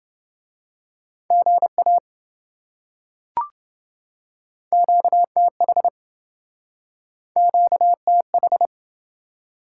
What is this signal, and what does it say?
Morse code, heard as a keyed tone, about 26 wpm, 700 Hz. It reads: GA QTH QTH